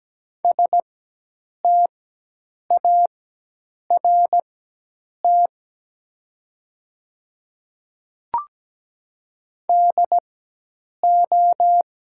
Morse code, heard as a keyed tone, about 17 wpm, spaced as for 8 wpm, 700 Hz. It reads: START DO
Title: 4x Character Spacing - Top 500 Words - 17wpm